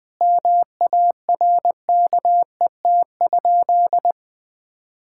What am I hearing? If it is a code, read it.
Morse code, heard as a keyed tone, about 21 wpm, 700 Hz. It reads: MARKET?
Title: Sentences from Top 1000 Words - Review Part 6 - 20wpm